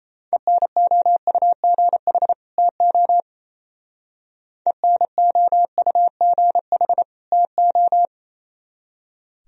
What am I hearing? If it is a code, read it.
Morse code, heard as a keyed tone, about 33 wpm, 700 Hz. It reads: ENOUGH TO ENOUGH TO
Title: Speed-Racing - Sets of 2 Words 28wpm